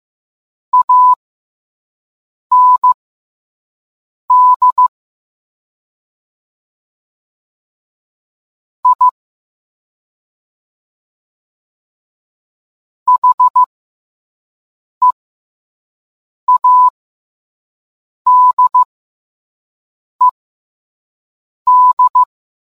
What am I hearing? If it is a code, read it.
Morse code, heard as a keyed tone, about 15 wpm, spaced as for 5 wpm, 1000 Hz. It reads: AND I HEADED